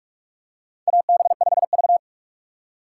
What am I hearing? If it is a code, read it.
Morse code, heard as a keyed tone, about 46 wpm, 700 Hz. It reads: AB5V
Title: USA Call Signs - 4 Characters - 6x Character Spacing - 45wpm